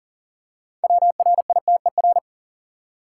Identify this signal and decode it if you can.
Morse code, heard as a keyed tone, about 37 wpm, 700 Hz. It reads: WRITER
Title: Six Letter Words - 40wpm